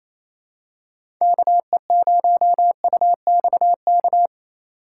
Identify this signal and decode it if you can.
Morse code, heard as a keyed tone, about 27 wpm, 700 Hz. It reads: KE0UXK
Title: USA Call Signs - 6 Characters - 8x Character Spacing - 28wpm